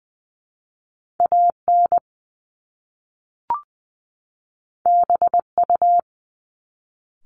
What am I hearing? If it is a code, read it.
Morse code, heard as a keyed tone, about 20 wpm, 700 Hz. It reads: A N B U
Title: Top Two Letter Combinations in Words 20wpm